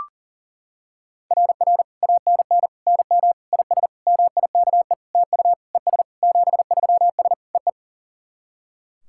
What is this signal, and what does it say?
Morse code, heard as a keyed tone, about 40 wpm, 700 Hz. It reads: RR ANN NM IS MIKE TU ES 73S EE
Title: SOTA QSO Practice 40wpm